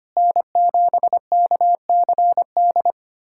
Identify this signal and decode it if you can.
Morse code, heard as a keyed tone, about 25 wpm, 700 Hz. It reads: N7KCD
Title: USA Callsign DE Callsign - 5 Characters - 25wpm